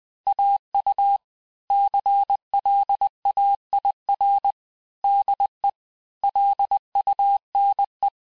Morse code, heard as a keyed tone, about 20 wpm, 800 Hz. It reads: AU CLAIR DE LUNE
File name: cours-du-mercredi / audios / cours-870-texte-CLAIR-vitesse-19-MPM